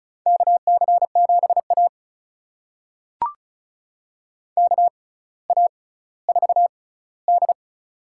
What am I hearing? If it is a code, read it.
Morse code, heard as a keyed tone, about 36 wpm, spaced as for 18 wpm, 700 Hz. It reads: KC7A KA4D
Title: USA Call Signs - 4 Characters - 6x Character Spacing - 35wpm